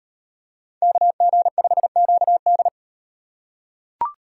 Morse code, heard as a keyed tone, about 38 wpm, 700 Hz. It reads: KG5QD